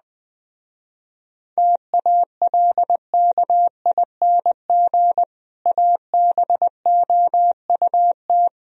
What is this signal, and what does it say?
Morse code, heard as a keyed tone, about 20 wpm, 700 Hz. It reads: TALKING ABOUT
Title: Speed-Racing - Sets of 2 Words 17wpm